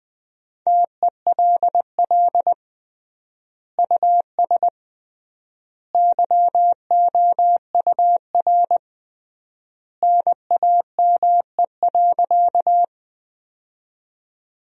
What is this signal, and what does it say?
Morse code, heard as a keyed tone, about 20 wpm, 700 Hz. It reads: TELL US YOUR NAME.
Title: 3x Word-Spacing - Sentences from Top 200 Words - 20wpm